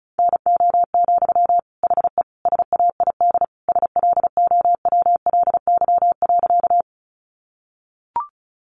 Morse code, heard as a keyed tone, about 34 wpm, 700 Hz. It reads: NO, HE SAID SLOWLY.
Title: Sentences from Top 1100 Words 35wpm